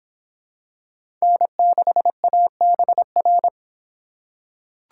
Morse code, heard as a keyed tone, about 24 wpm, 700 Hz. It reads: N6ABR